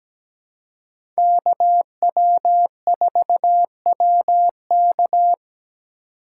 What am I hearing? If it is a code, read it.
Morse code, heard as a keyed tone, about 17 wpm, 700 Hz. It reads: KW4WK
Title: USA Call Signs - 5 Characters - 4x Character Spacing - 17wpm